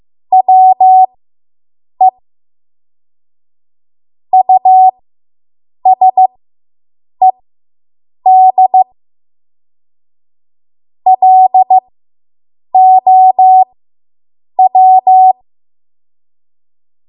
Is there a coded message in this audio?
Yes: Morse code from 0.3 to 15.3 s, heard as a keyed tone, about 15 wpm, spaced as for 7 wpm, 750 Hz. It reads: WE USED LOW